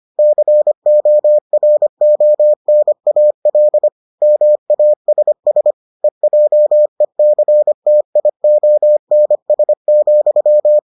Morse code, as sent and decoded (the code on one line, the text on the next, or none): -.-. --- .-. --- -. .- .-.. / -- .- ... ... / . .--- . -.-. - .. --- -. ... --..--
CORONAL MASS EJECTIONS,